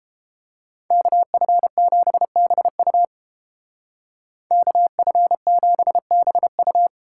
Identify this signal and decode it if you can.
Morse code, heard as a keyed tone, about 29 wpm, 700 Hz. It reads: KF7BU KF7BU